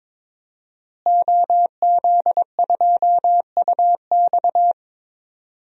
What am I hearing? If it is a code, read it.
Morse code, heard as a keyed tone, about 21 wpm, 700 Hz. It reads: OZ2UX
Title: DXCC-100 Call Signs - 5 Characters - Repeated 3x - 22wpm